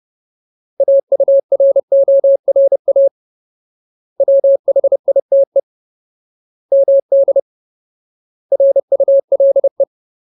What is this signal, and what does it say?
Morse code, heard as a keyed tone, about 30 wpm, 550 Hz. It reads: AURORA WHITE MD RULE
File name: Ultimate Word List / 30WPM Ultimate